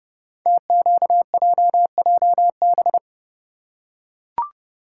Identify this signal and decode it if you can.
Morse code, heard as a keyed tone, about 30 wpm, 700 Hz. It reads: TQJJB
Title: Code Groups - 5 Letter:Numbers - 30wpm